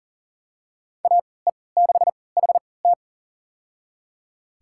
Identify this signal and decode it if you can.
Morse code, heard as a keyed tone, about 40 wpm, spaced as for 23 wpm, 700 Hz. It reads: AE6HT
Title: USA Call Signs - 5 Characters - 3x Character Spacing - 40wpm